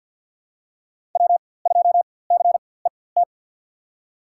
Morse code, heard as a keyed tone, about 48 wpm, spaced as for 23 wpm, 700 Hz. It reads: W2CET
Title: USA Call Signs - 5 Characters - 4x Character Spacing - 50wpm